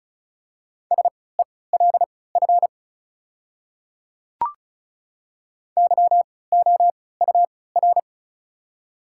Morse code, heard as a keyed tone, about 36 wpm, spaced as for 20 wpm, 700 Hz. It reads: SELF YOUR
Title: 3x Character Spacing - Top 300 Words - 35wpm